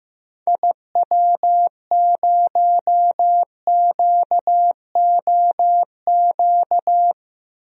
Morse code, heard as a keyed tone, about 15 wpm, 700 Hz. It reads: IW0QOQ
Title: DXCC-100 Call Signs - 6 Characters - 15wpm